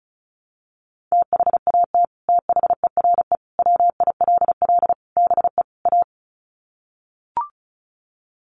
Morse code, heard as a keyed tone, about 34 wpm, 700 Hz. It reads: THAT THERE WILL BE A
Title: Sets of 5 Words Encore2 35wpm